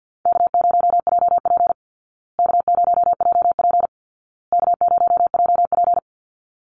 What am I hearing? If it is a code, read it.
Morse code, heard as a keyed tone, about 51 wpm, 700 Hz. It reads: K0JP K0JP K0JP
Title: Easy Call Signs Repeated 3x - 50wpm